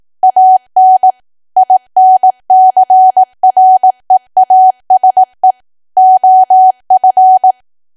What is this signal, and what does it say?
Morse code, heard as a keyed tone, about 18 wpm, 750 Hz. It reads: AN INCREASE OF
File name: cwa / W1AW-Level2 / 18WPM